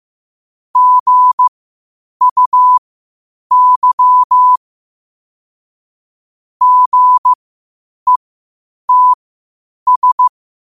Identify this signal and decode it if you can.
Morse code, heard as a keyed tone, about 15 wpm, spaced as for 8 wpm, 1000 Hz. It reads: GUY GETS